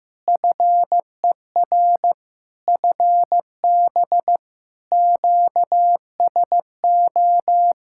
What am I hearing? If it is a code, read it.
Morse code, heard as a keyed tone, about 15 wpm, 700 Hz. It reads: FER FB QSO
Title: Practice QSO - Exchange 4 - Ending - 15wpm